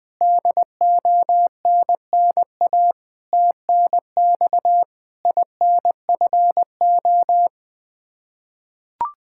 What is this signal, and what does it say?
Morse code, heard as a keyed tone, about 21 wpm, 700 Hz. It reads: DONNA TNX INFO
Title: Practice QSO - Exchange 3 - More Info - 20wpm